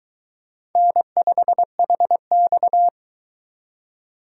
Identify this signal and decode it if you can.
Morse code, heard as a keyed tone, about 23 wpm, 700 Hz. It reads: N5HX